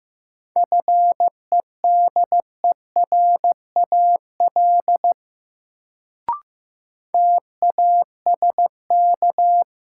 Morse code, heard as a keyed tone, about 15 wpm, 700 Hz. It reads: FEDERAL TASK
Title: Sentences from Top 1100 Words - New Word Drill Alt - 15wpm